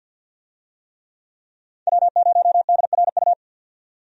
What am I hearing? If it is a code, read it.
Morse code, heard as a keyed tone, about 48 wpm, 700 Hz. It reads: W0DRU
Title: USA Call Signs - 5 Characters - 50wpm